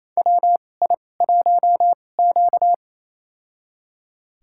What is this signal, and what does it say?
Morse code, heard as a keyed tone, about 28 wpm, spaced as for 20 wpm, 700 Hz. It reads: WI1Q
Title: USA Call Signs - 4 Characters - 2x Character Spacing - 28wpm